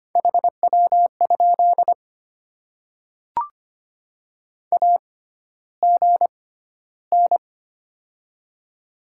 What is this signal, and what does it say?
Morse code, heard as a keyed tone, about 25 wpm, spaced as for 13 wpm, 700 Hz. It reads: HW? AGN
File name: Top 20 QSO Elements - 6x Character Spacing - 25wpm